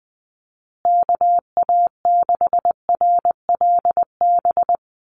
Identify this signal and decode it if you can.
Morse code, heard as a keyed tone, about 20 wpm, 700 Hz. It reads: KA6RLB
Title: USA Call Signs Repeated 3x 20wpm